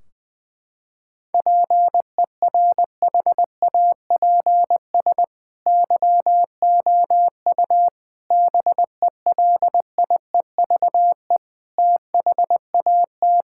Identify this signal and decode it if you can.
Morse code, heard as a keyed tone, about 20 wpm, 700 Hz. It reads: PERHAPS YOU BELIEVE THAT
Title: Sentences from Top 1600 Words 20wpm